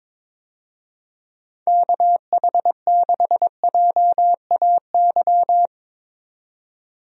Morse code, heard as a keyed tone, about 22 wpm, 700 Hz. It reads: KH6JAY
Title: DXCC-100 Call Signs - 6 Characters - 22wpm